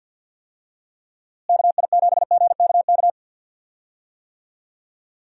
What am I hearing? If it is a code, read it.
Morse code, heard as a keyed tone, about 51 wpm, 700 Hz. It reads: KI7GKK
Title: USA Call Signs - 6 Characters - Repeated 3x - 50wpm